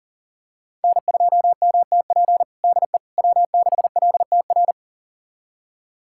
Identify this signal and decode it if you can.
Morse code, heard as a keyed tone, about 40 wpm, 700 Hz. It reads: N2MTP DE W6LTR
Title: USA Callsign DE Callsign - 5 Characters - 40wpm